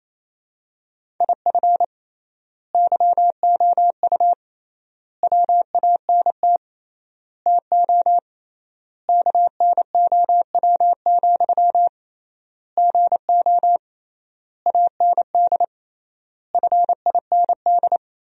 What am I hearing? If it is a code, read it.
Morse code, heard as a keyed tone, about 28 wpm, 700 Hz. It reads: IF YOU WANT TO KNOW, GO AND FIND
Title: 3x Word-Spacing - Sentences from Top 200 Words - 28wpm